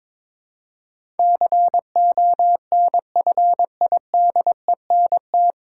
Morse code, heard as a keyed tone, about 22 wpm, 700 Hz. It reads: CONFIDENT
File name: 9 Letter Words - First 3 Letters Given - 22wpm